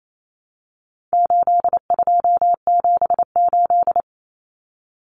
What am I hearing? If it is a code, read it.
Morse code, heard as a keyed tone, about 28 wpm, 700 Hz. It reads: 8278
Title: Numbers - 4-Digit 28wpm